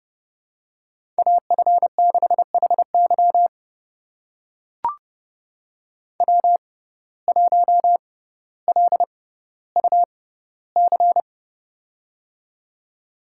Morse code, heard as a keyed tone, about 30 wpm, spaced as for 15 wpm, 700 Hz. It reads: AF6HY W1LUC
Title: USA Call Signs - 5 Characters - 6x Character Spacing - 30wpm